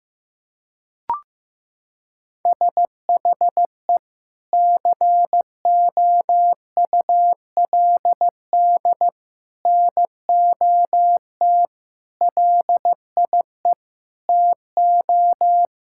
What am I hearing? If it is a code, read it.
Morse code, heard as a keyed tone, about 15 wpm, 700 Hz. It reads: SHE COULD NOT LIE TO
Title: Sentences from Top 1600 Words 15wpm